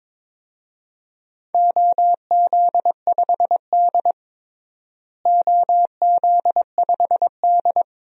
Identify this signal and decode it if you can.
Morse code, heard as a keyed tone, about 22 wpm, 700 Hz. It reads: OZ5D OZ5D
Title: DXCC-100 Call Signs - 4 Characters - Repeated 2x - 22wpm